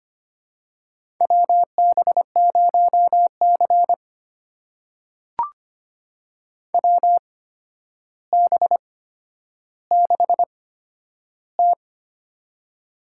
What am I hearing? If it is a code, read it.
Morse code, heard as a keyed tone, about 25 wpm, spaced as for 11 wpm, 700 Hz. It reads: WB0C WB6T
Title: USA Call Signs - 4 Characters - 8x Character Spacing - 25wpm